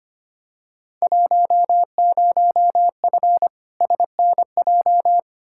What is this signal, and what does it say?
Morse code, heard as a keyed tone, about 25 wpm, 700 Hz. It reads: 10F SNJ
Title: Field Day Exchanges Encore - 25wpm